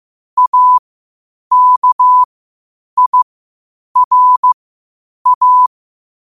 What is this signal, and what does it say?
Morse code, heard as a keyed tone, about 15 wpm, spaced as for 8 wpm, 1000 Hz. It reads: AKIRA
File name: qotdfiles / 8wpm